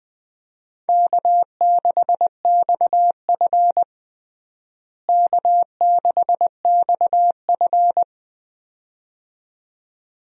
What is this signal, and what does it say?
Morse code, heard as a keyed tone, about 21 wpm, 700 Hz. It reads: K6XF K6XF
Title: USA Call Signs - 4 Characters - Repeated 2x - 20wpm